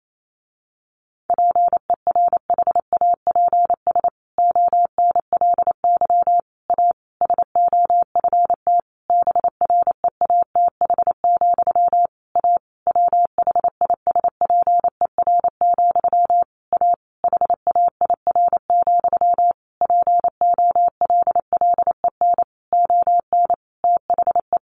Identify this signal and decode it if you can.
Morse code, heard as a keyed tone, about 28 wpm, 700 Hz. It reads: PERHAPS ONLY A SOFT BREATH, A WHISPER, A HAIR, POLLEN ON THE
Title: A Sound Of Thunder 28wpm